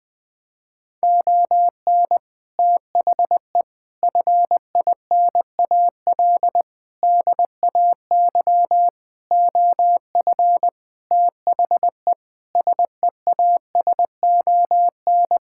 Morse code, heard as a keyed tone, about 20 wpm, 700 Hz. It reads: ON THE FINAL DAY OF THE SEASON
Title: Sentences with 7-Word Phrases - 20wpm